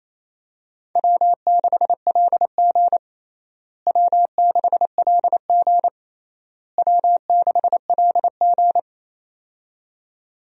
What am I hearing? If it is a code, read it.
Morse code, heard as a keyed tone, about 28 wpm, 700 Hz. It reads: W6LG W6LG W6LG
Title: USA Call Signs - 4 Characters - Repeated 3x - 28wpm